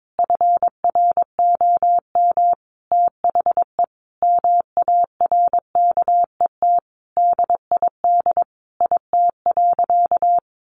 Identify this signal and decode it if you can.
Morse code, heard as a keyed tone, about 22 wpm, 700 Hz. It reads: FROM THE MARKET DID IT.
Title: Sentences from Top 600 Words 22wpm